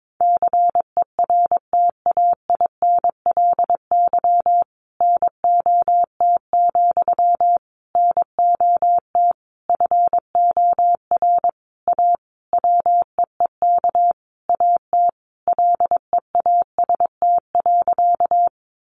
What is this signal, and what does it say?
Morse code, heard as a keyed tone, about 22 wpm, 700 Hz. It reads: CERTAINLY NOT, NOT FOR A WEEK AT LEAST.
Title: Sentences from Top 900 Words 22wpm